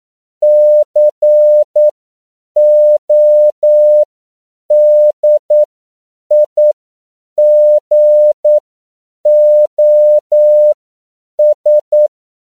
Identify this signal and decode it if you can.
Morse code, heard as a keyed tone, about 9 wpm, spaced as for 7 wpm, 600 Hz. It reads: CODIGOS
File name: CW / 10PPM